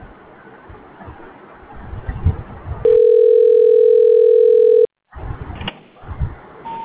Ringback tone (North America), 440 + 480 Hz.